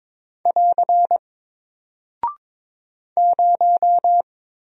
Morse code, heard as a keyed tone, about 22 wpm, 700 Hz. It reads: <AR> 0
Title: Single Character (Letter:Number:Punctuation:Prosign) 22wpm